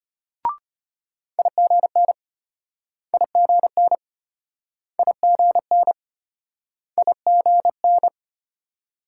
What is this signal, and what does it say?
Morse code, heard as a keyed tone, about 33 wpm, 700 Hz. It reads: IGN IGN IGN IGN